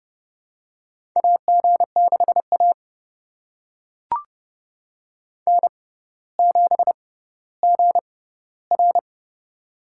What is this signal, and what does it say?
Morse code, heard as a keyed tone, about 29 wpm, spaced as for 15 wpm, 700 Hz. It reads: AG6A N7GR
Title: USA Call Signs - 4 Characters - 6x Character Spacing - 30wpm